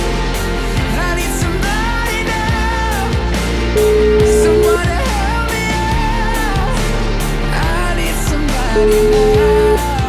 Ringback tone (Europe, 425 Hz), with noise.